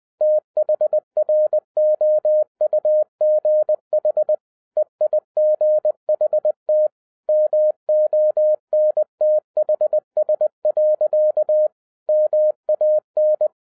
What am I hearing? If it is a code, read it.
Morse code, heard as a keyed tone, about 20 wpm, 610 Hz. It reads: THROUGH EIGHT MONTHS. MAN
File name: 2012-01-mp3 / n16-THROUGH EIGHT MONTHS-20wpm